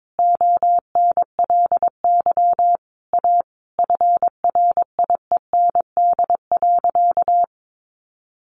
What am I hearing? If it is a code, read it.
Morse code, heard as a keyed tone, about 22 wpm, 700 Hz. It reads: ONLY A FRIEND.
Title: Sentences from Top 1000 Words 22wpm